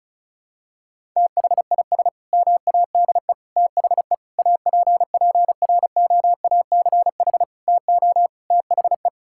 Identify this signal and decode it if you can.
Morse code, heard as a keyed tone, about 36 wpm, 700 Hz. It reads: THIS MADE THE APPROACH TO THE
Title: Sentences from Top 1000 Words - Review Part 6 - 35wpm